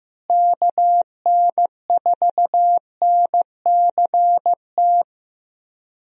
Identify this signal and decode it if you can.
Morse code, heard as a keyed tone, about 15 wpm, 700 Hz. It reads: KN4NCT
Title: USA Call Signs - 6 Characters - Repeated 2x - 15wpm